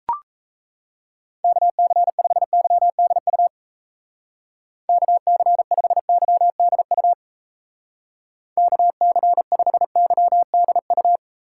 Morse code, heard as a keyed tone, about 34 wpm, 700 Hz. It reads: KC5YDU KC5YDU KC5YDU
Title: Speed Racing - USA 2x3 Call Signs 28wpm